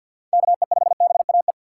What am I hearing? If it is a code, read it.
Morse code, heard as a keyed tone, about 52 wpm, 690 Hz. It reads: KE5BAE
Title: USA Call Signs - 3x Character Spacing - 50wpm